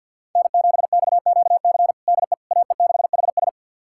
Morse code, heard as a keyed tone, about 47 wpm, 690 Hz. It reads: N7XQC DE AE6HS
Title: USA Callsign DE Callsign - 5 Characters - 50wpm